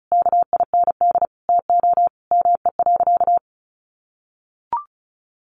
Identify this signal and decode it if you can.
Morse code, heard as a keyed tone, about 34 wpm, 700 Hz. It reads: KIND TO ME.
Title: Sentences from Top 1100 Words 35wpm